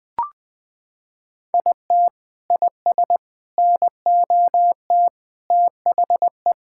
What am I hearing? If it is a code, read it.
Morse code, heard as a keyed tone, about 20 wpm, 700 Hz. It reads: IT IS NOT THE
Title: Sentences with 3-Word Phrases - 20wpm